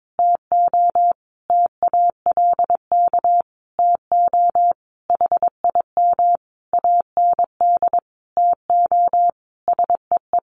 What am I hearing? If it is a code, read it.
Morse code, heard as a keyed tone, about 22 wpm, 700 Hz. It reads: TO TALK TO HIM AND TO SEE
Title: Sentences from Top 1400 Words 22wpm